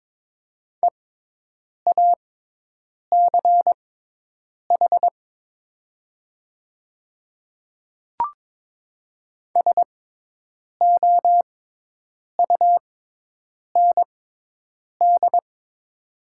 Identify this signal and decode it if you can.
Morse code, heard as a keyed tone, about 22 wpm, spaced as for 8 wpm, 700 Hz. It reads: EACH SOUND
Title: 6x Character Spacing - Top 200 Words - 22wpm